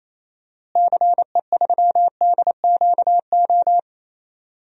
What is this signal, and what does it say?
Morse code, heard as a keyed tone, about 28 wpm, 700 Hz. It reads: CE3DQO
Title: DXCC-100 Call Signs - 6 Characters - Repeated 2x - 28wpm